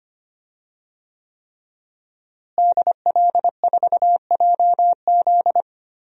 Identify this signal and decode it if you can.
Morse code, heard as a keyed tone, about 25 wpm, 700 Hz. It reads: DL4JZ
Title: DXCC-100 Call Signs - 5 Characters - Repeated 3x - 25wpm